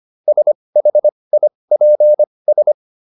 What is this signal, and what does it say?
Morse code, heard as a keyed tone, about 25 wpm, spaced as for 20 wpm, 600 Hz. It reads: SHIPS